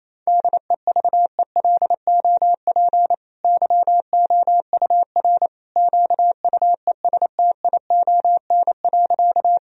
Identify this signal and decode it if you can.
Morse code, heard as a keyed tone, about 27 wpm, 700 Hz. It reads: DEVELOP YOUR QUESTION.
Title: Sentences from Top 1000 Words - Review Part 2 - 28wpm